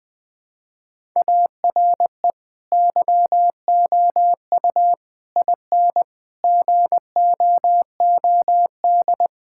Morse code, heard as a keyed tone, about 21 wpm, 700 Hz. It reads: ARE YOU IN GOOD
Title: Sentences from Top 1000 Words - Review Part 5 - 20wpm